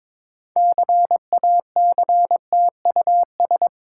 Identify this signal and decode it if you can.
Morse code, heard as a keyed tone, about 22 wpm, 690 Hz. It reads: CACTUS
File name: Difficult Words with Hints - 22wpm